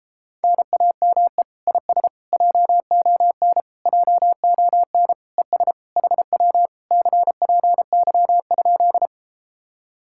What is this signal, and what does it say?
Morse code, heard as a keyed tone, about 33 wpm, 700 Hz. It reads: NAME IS JON JON ES HW CPY?